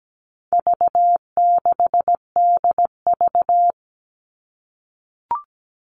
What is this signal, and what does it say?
Morse code, heard as a keyed tone, about 17 wpm, 700 Hz. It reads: V6DV
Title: All Cut Numbers - 4-Digit 17wpm